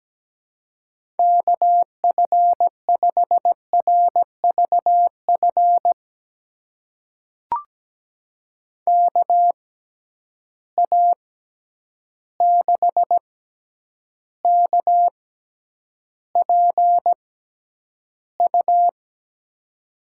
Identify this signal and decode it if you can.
Morse code, heard as a keyed tone, about 17 wpm, spaced as for 9 wpm, 700 Hz. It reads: KF5RVF KA6KPU
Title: USA Call Signs - 6 Characters - 6x Character Spacing - 17wpm